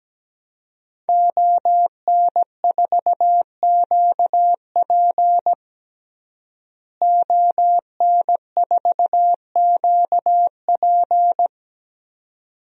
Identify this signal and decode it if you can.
Morse code, heard as a keyed tone, about 17 wpm, 700 Hz. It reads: ON4QP ON4QP
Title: DXCC-100 Call Signs - 5 Characters - Repeated 2x - 17wpm